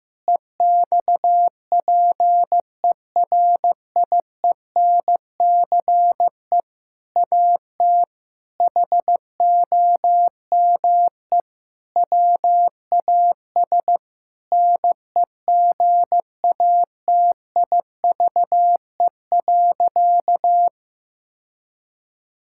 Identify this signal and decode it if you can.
Morse code, heard as a keyed tone, about 15 wpm, 700 Hz. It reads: EXPERIENCE AT HOME WAS NEGATIVE.
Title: Sentences from Top 1000 Words - Review Part 6 - 15wpm